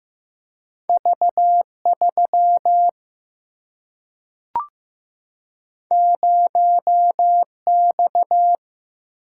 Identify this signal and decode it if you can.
Morse code, heard as a keyed tone, about 15 wpm, 700 Hz. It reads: V3 0X
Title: Single-Letter Single-Number 15wpm